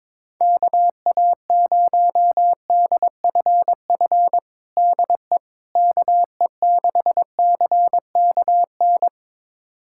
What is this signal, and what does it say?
Morse code, heard as a keyed tone, about 22 wpm, 700 Hz. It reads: KA0DFF DE KE6CKN
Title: USA Callsign DE Callsign 22wpm